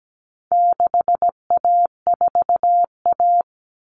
Morse code, heard as a keyed tone, about 17 wpm, 700 Hz. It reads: 6A4A
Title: Common Cut Numbers - 4-Digit 17wpm